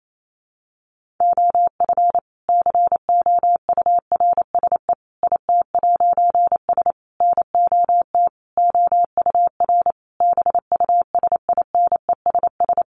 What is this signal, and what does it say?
Morse code, heard as a keyed tone, about 28 wpm, 700 Hz. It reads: OF COURSE IT'S NOT OUR BUSINESS
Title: A Sound Of Thunder 28wpm